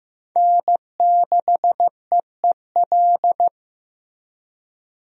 Morse code, heard as a keyed tone, about 15 wpm, 700 Hz. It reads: N6EEL
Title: USA Call Signs - 5 Characters - Repeated 2x - 15wpm